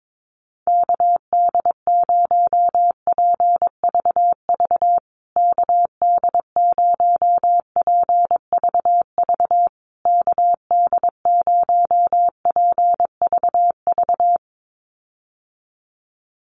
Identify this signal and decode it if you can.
Morse code, heard as a keyed tone, about 22 wpm, 700 Hz. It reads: KD0PVV KD0PVV KD0PVV
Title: USA Call Signs Repeated 3x 22wpm